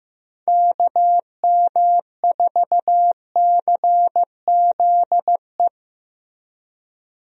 Morse code, heard as a keyed tone, about 14 wpm, 700 Hz. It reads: KM4CZE
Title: USA Call Signs - 6 Characters - Repeated 3x - 15wpm